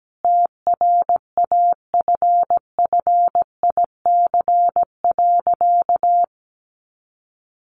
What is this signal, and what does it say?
Morse code, heard as a keyed tone, about 17 wpm, 700 Hz. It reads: TRAFFIC.
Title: Sentences from Top 1300 Words 17wpm